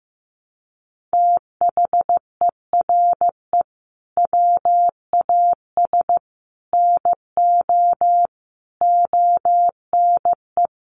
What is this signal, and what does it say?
Morse code, heard as a keyed tone, about 15 wpm, 700 Hz. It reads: THERE WAS NO ONE
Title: Sentences from Top 200 Words 15wpm